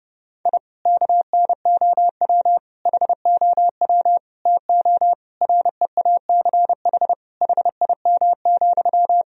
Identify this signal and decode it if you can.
Morse code, heard as a keyed tone, about 30 wpm, 700 Hz. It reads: I KNOW HOW TO REACH HIM,
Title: Sentences from Top 1000 Words - Review Part 2 - 30wpm